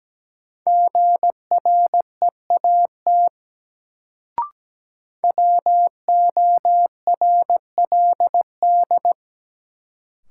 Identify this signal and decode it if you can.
Morse code, heard as a keyed tone, about 17 wpm, 700 Hz. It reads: GREAT WORLD
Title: Sets of 2 Words - New Word Drill - 17wpm